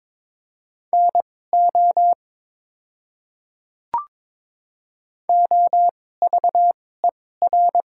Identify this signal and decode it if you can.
Morse code, heard as a keyed tone, about 23 wpm, spaced as for 16 wpm, 700 Hz. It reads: NO OVER
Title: 2x Character Spacing - Top 100 Words - 22wpm